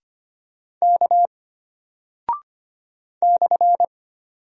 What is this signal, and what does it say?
Morse code, heard as a keyed tone, about 25 wpm, 700 Hz. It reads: K /
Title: Single Character (Letter:Number:Punctuation:Prosign) Alt 25wpm